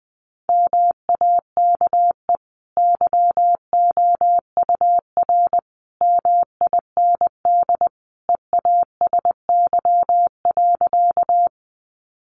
Morse code, heard as a keyed tone, about 20 wpm, 700 Hz. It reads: MAKE YOUR MIND EASY.
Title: Sentences from Top 700 Words 20wpm